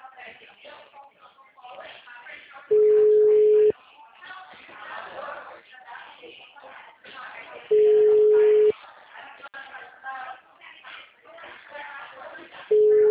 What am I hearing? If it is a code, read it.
Ringback tone (Europe, 425 Hz), with speech.